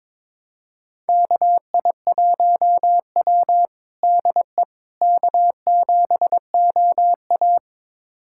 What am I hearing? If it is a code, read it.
Morse code, heard as a keyed tone, about 22 wpm, 700 Hz. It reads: KI1W DE K7OA
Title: USA Callsign DE Callsign - 4 Characters - 22wpm